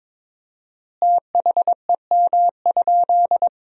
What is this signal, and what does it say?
Morse code, heard as a keyed tone, about 22 wpm, 700 Hz. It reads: THEM?
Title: 4x Word-Spacing - Sentences from Top 100 Words - 22wpm